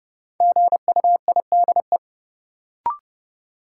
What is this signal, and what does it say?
Morse code, heard as a keyed tone, about 30 wpm, 700 Hz. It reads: GUIDE
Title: Sentences from Top 600 Words - New Word Drill - 30wpm